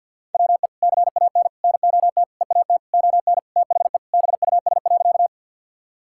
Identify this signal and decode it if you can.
Morse code, heard as a keyed tone, about 48 wpm, 700 Hz. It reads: WE CAN NOT EAT ON THE BUS.
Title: Sentences from Top 1000 Words - Review Part 6 - 50wpm